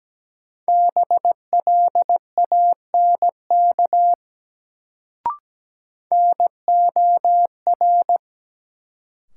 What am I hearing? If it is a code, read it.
Morse code, heard as a keyed tone, about 17 wpm, 700 Hz. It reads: BLANK NOR
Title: Sentences from Top 1500 Words - New Word Drill - 17wpm